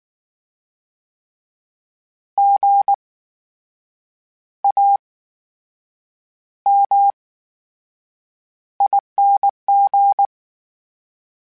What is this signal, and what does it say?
Morse code, heard as a keyed tone, about 19 wpm, spaced as for 7 wpm, 800 Hz. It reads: GAMING
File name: Special INGs / Special INGs-mp3s / VerySlowToFaster Ing @18